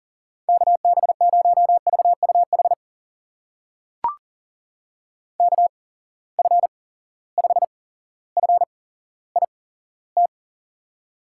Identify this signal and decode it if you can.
Morse code, heard as a keyed tone, about 40 wpm, spaced as for 17 wpm, 700 Hz. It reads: KB0VUH KF5FIT
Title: USA Call Signs - 8x Character Spacing - 40wpm